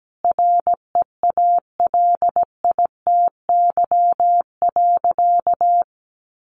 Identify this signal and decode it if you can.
Morse code, heard as a keyed tone, about 17 wpm, 700 Hz. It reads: REALITY.
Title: Sentences from Top 1500 Words 17wpm